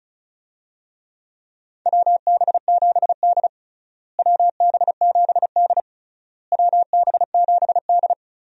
Morse code, heard as a keyed tone, about 36 wpm, 700 Hz. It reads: WB7D WB7D WB7D
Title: USA Call Signs - 4 Characters - Repeated 3x - 35wpm